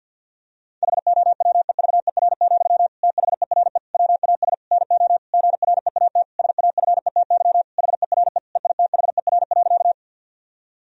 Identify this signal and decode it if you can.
Morse code, heard as a keyed tone, about 50 wpm, 700 Hz. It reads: HOWEVER, THERE WAS NO GREAT SAFETY HERE EITHER.
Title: Sentences from Top 1000 Words - Review Part 4 - 50wpm